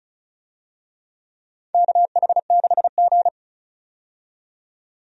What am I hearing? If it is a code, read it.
Morse code, heard as a keyed tone, about 34 wpm, 700 Hz. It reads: KH6G
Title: DXCC-100 Call Signs - 4 Characters - 35wpm